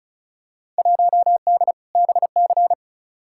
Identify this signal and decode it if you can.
Morse code, heard as a keyed tone, about 36 wpm, 700 Hz. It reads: 1D BC
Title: Field Day Exchanges - 35wpm